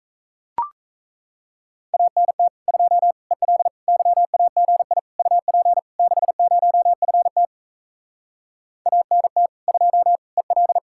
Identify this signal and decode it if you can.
Morse code, heard as a keyed tone, about 38 wpm, 700 Hz. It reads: ANT 2 EL YAGI UP 60FT ANT 2 EL